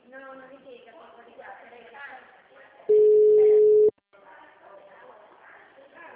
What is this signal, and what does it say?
Ringback tone (Europe, 425 Hz), with speech.
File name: out- -2010- - - .96701